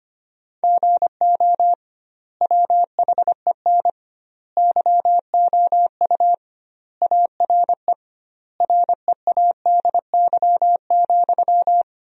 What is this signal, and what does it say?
Morse code, heard as a keyed tone, about 25 wpm, 700 Hz. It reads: GO WHEN YOU ARE READY,
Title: 2x Word-Spacing - Sentences from Top 500 Words - 25wpm